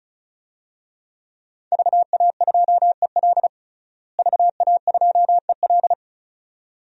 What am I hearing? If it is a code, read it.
Morse code, heard as a keyed tone, about 36 wpm, 700 Hz. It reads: VA2EL VA2EL
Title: DXCC-100 Call Signs - 5 Characters - Repeated 2x - 35wpm